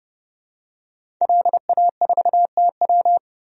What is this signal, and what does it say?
Morse code, heard as a keyed tone, about 31 wpm, 700 Hz. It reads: LA4TW
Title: DXCC-100 Call Signs - 5 Characters - 30wpm